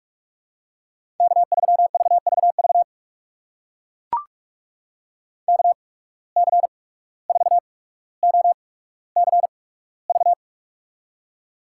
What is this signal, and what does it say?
Morse code, heard as a keyed tone, about 43 wpm, spaced as for 18 wpm, 700 Hz. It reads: K3VVV KC4OCV
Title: USA Call Signs - 8x Character Spacing - 45wpm